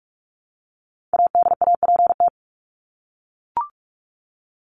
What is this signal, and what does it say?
Morse code, heard as a keyed tone, about 44 wpm, 700 Hz. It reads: ADAPT
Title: Top 1501-1750 Words 45wpm